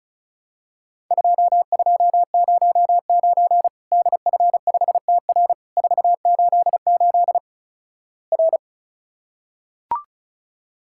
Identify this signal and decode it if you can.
Morse code, heard as a keyed tone, about 36 wpm, 700 Hz. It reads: 2209 DF5TR 488 R
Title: Worked All Europe DX Contest - QTC Exchange - 35wpm